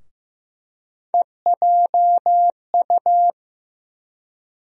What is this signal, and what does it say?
Morse code, heard as a keyed tone, about 15 wpm, 700 Hz. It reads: EJU
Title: USA Call Sign Postfixes - 3 Letters 15wpm